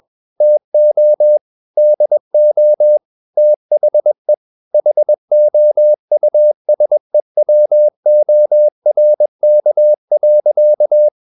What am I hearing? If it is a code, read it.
Morse code, heard as a keyed tone, about 21 wpm, 600 Hz. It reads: TO DO THE HOUSEWORK.
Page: CW med Gnister